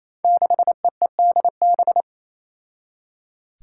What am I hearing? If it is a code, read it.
Morse code, heard as a keyed tone, about 28 wpm, 700 Hz. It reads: 6EEDB
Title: Five Challenging Letter-Number Combinations 28wpm